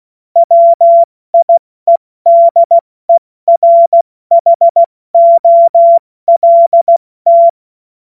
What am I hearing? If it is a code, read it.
Morse code, heard as a keyed tone, about 16 wpm, 680 Hz. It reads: WIEDERHOLT
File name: Morsekurs / Witze / Witz 12 März 2024 70 F80